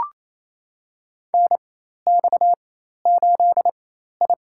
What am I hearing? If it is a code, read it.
Morse code, heard as a keyed tone, about 28 wpm, spaced as for 13 wpm, 700 Hz. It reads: NX8I
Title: USA Call Signs - 4 Characters - 4x Character Spacing - 28wpm